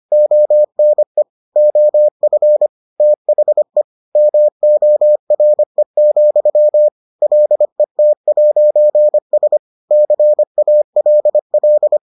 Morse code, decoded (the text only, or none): ONE OF THE MORE, LET'S CALL